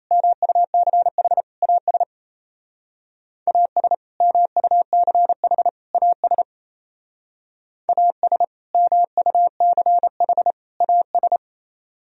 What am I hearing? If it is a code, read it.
Morse code, heard as a keyed tone, about 35 wpm, 700 Hz. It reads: MUCH AS AS MUCH AS AS MUCH AS